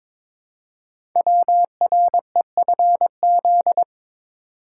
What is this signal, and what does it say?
Morse code, heard as a keyed tone, about 22 wpm, 700 Hz. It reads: WREFZ
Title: Code Groups - 5 Letter:Numbers - 22wpm